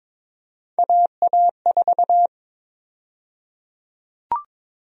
Morse code, heard as a keyed tone, about 22 wpm, 700 Hz. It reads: AA4
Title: USA Call Sign Prefixes - 22wpm